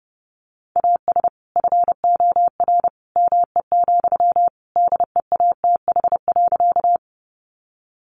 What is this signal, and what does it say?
Morse code, heard as a keyed tone, about 30 wpm, 700 Hz. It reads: AS FOR ME, DEATH.
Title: Sentences from Top 1100 Words 30wpm